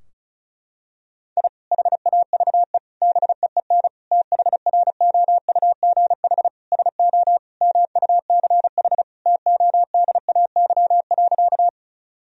Morse code, heard as a keyed tone, about 36 wpm, 700 Hz. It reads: I HAVE BEEN THROUGH SO MUCH TODAY.
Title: Sentences from Top 1000 Words - Review Part 4 - 35wpm